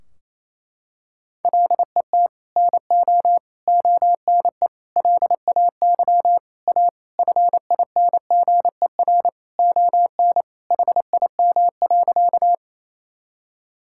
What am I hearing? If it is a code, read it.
Morse code, heard as a keyed tone, about 28 wpm, 700 Hz. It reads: LET NO ONE LAY A FINGER ON HIM.
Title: Sentences from Top 1600 Words 28wpm